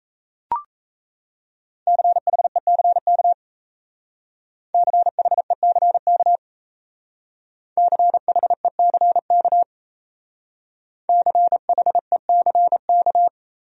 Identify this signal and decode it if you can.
Morse code, heard as a keyed tone, about 34 wpm, 700 Hz. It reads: CHECK CHECK CHECK CHECK
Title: Speed-Racing Top 500 Words 28wpm